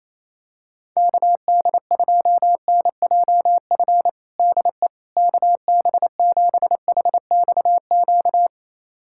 Morse code, heard as a keyed tone, about 27 wpm, 700 Hz. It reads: KD2NJF DE KB7HXQ
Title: USA Callsign DE Callsign - 6 Characters - 28wpm